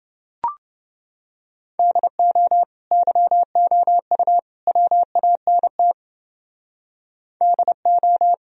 Morse code, heard as a keyed tone, about 30 wpm, 700 Hz. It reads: DO YOU WANT DO